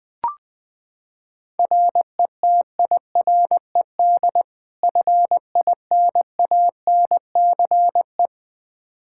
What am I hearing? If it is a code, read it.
Morse code, heard as a keyed tone, about 20 wpm, 700 Hz. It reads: RETIRED FINANCE
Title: K1USN SST QSO Practice 20wpm